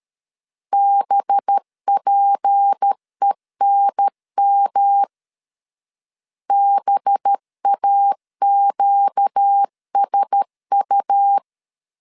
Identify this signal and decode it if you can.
Morse code, heard as a keyed tone, about 13 wpm, 800 Hz. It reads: BPENM BAQSU
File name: CW / DET / 1 6 1500C S A 12wpm 3440